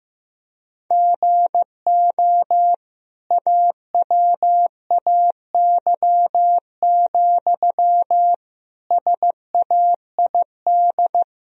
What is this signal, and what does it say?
Morse code, heard as a keyed tone, about 15 wpm, 700 Hz. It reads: GO AWAY, SAID
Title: Sentences from Top 1000 Words - Review Part 2 - 15wpm